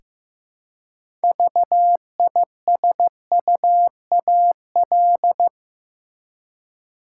Morse code, heard as a keyed tone, about 15 wpm, 700 Hz. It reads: VISUAL
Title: Six Letter Words - 15wpm